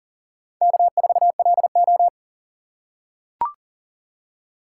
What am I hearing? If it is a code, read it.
Morse code, heard as a keyed tone, about 40 wpm, 700 Hz. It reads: K4LO
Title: USA Call Signs - 2x Character Spacing - 40wpm